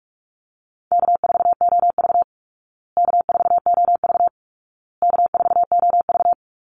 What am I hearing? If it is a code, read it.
Morse code, heard as a keyed tone, about 44 wpm, 700 Hz. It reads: K4OV K4OV K4OV
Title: Easy Call Signs Repeated 3x - 45wpm